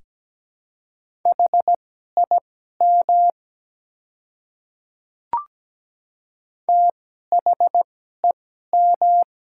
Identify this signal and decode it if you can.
Morse code, heard as a keyed tone, about 17 wpm, spaced as for 12 wpm, 700 Hz. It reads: HIM THEM
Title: 2x Character Spacing - Top 200 Words - 17wpm